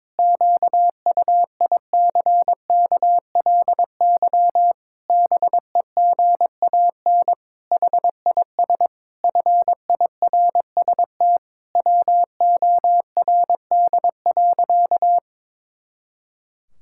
Morse code, heard as a keyed tone, about 22 wpm, 700 Hz. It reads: QUICKLY BEGAN HIS FIRST WORD.
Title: Pangram Challenge v2 - 22wpm